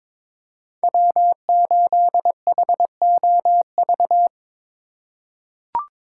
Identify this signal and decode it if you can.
Morse code, heard as a keyed tone, about 22 wpm, 700 Hz. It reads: W8HOV
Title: USA Call Signs - 4x Character Spacing - 22wpm